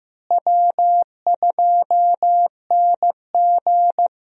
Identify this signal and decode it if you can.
Morse code, heard as a keyed tone, about 15 wpm, 700 Hz. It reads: W2NG
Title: USA Call Signs - 4 Characters - Repeated 3x - 15wpm